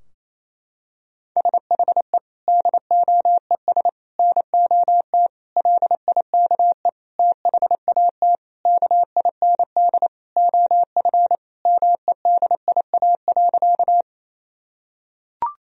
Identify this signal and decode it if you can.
Morse code, heard as a keyed tone, about 28 wpm, 700 Hz. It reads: SHE DOES NOT LIKE THAT KIND OF MEDIA.
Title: Sentences from Top 1000 Words - Review Part 4 - 28wpm